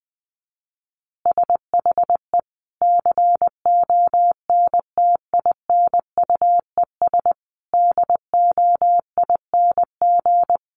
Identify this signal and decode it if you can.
Morse code, heard as a keyed tone, about 20 wpm, 700 Hz. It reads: SHE CONTINUES DOING
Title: Easy Phrases 20wpm